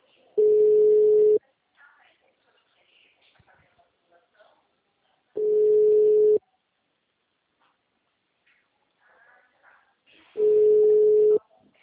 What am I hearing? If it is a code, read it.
Ringback tone (Europe, 425 Hz).
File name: out- -2005- - - .84603